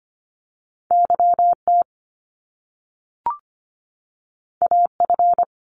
Morse code, heard as a keyed tone, about 25 wpm, 700 Hz. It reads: YT AF
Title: Top Two Letter Combinations in Words 25wpm